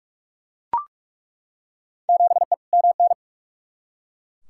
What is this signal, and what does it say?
Morse code, heard as a keyed tone, about 42 wpm, 700 Hz. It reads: 7E MN
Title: Field Day Exchanges Encore - 45wpm